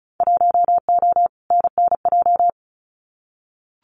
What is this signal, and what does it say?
Morse code, heard as a keyed tone, about 34 wpm, 700 Hz. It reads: 1O NNJ
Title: Winter Field Day Exchanges 35wpm